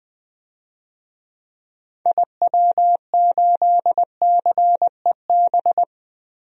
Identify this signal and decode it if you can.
Morse code, heard as a keyed tone, about 20 wpm, 700 Hz. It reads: IW8CEB
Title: DXCC-100 Call Signs - 6 Characters - Repeated 2x - 20wpm